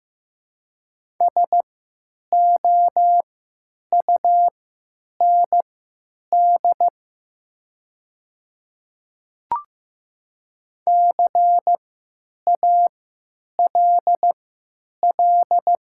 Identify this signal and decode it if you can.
Morse code, heard as a keyed tone, about 15 wpm, spaced as for 9 wpm, 700 Hz. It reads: SOUND CALL
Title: 3x Character Spacing - Top 300 Words - 15wpm